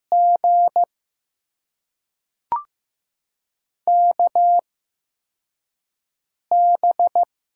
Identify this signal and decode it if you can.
Morse code, heard as a keyed tone, about 16 wpm, spaced as for 4 wpm, 700 Hz. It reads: G KB
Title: USA Call Signs - 6 Characters - 8x Character Spacing - 15wpm